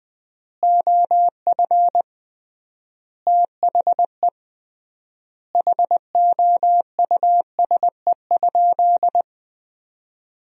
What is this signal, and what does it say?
Morse code, heard as a keyed tone, about 20 wpm, 700 Hz. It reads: OF THE HOUSE?
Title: 3x Word-Spacing - Sentences from Top 200 Words - 20wpm